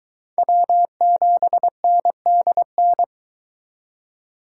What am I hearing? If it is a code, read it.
Morse code, heard as a keyed tone, about 23 wpm, 700 Hz. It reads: W7NDN